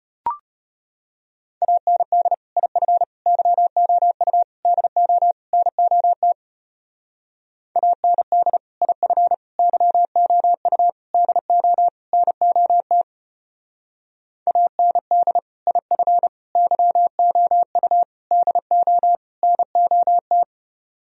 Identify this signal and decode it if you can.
Morse code, heard as a keyed tone, about 31 wpm, 700 Hz. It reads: AND IF YOU DO NOT AND IF YOU DO NOT AND IF YOU DO NOT
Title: Speed-Racing - Sets of 5 Words 25wpm